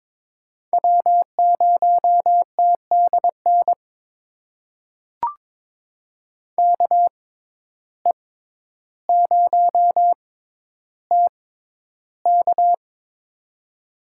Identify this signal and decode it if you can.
Morse code, heard as a keyed tone, about 22 wpm, spaced as for 11 wpm, 700 Hz. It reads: W0TDN KE0TK
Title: USA Call Signs - 5 Characters - 6x Character Spacing - 22wpm